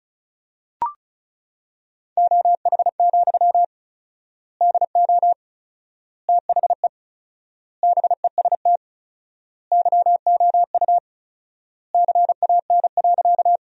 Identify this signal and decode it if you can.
Morse code, heard as a keyed tone, about 36 wpm, 700 Hz. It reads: OH, DO THE BEST YOU CAN.
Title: 4x Word-Spacing - Sentences from Top 500 Words - 35wpm